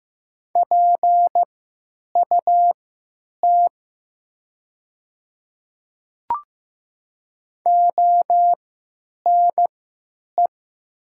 Morse code, heard as a keyed tone, about 15 wpm, spaced as for 9 wpm, 700 Hz. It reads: PUT ONE
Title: 3x Character Spacing - Top 200 Words - 15wpm